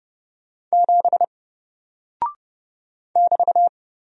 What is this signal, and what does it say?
Morse code, heard as a keyed tone, about 30 wpm, 700 Hz. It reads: 7 <BT>
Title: Single Character (Letter:Number:Punctuation:Prosign) 30wpm